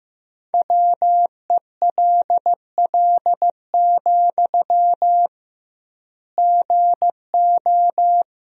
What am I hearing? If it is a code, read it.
Morse code, heard as a keyed tone, about 15 wpm, 700 Hz. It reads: WELL, GO
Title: 2x Word-Spacing - Sentences from Top 200 Words - 15wpm